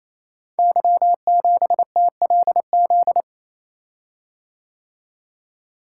Morse code, heard as a keyed tone, about 28 wpm, 700 Hz. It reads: Y7TLZ
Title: Code Groups - 5 Letter:Numbers - 28wpm